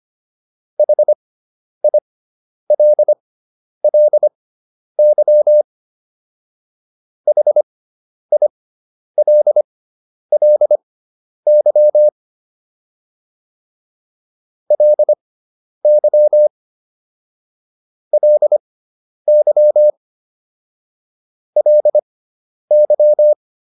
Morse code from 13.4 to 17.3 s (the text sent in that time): LY